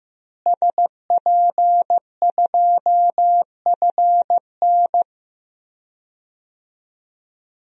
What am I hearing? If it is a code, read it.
Morse code, heard as a keyed tone, about 15 wpm, 700 Hz. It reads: SP2FN
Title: DXCC-100 Call Signs - 5 Characters - Repeated 2x - 15wpm